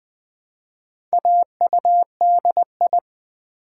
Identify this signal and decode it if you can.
Morse code, heard as a keyed tone, about 20 wpm, 700 Hz. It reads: AUDI
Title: Build A Word - 1 Letter At A Time - 5-Letter Words - 20wpm